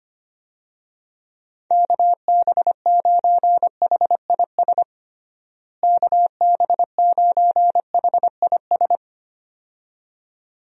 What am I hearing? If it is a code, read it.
Morse code, heard as a keyed tone, about 25 wpm, 700 Hz. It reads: KB9HIS KB9HIS
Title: USA Call Signs - 6 Characters - Repeated 2x - 25wpm